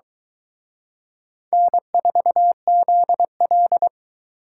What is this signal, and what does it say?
Morse code, heard as a keyed tone, about 23 wpm, 700 Hz. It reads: N4ZL